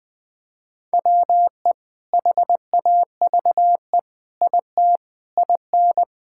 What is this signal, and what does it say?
Morse code, heard as a keyed tone, about 21 wpm, 700 Hz. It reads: WE HAVE IT IN
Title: Sentences with 4-Word Phrases - Spoken Beginning - 20wpm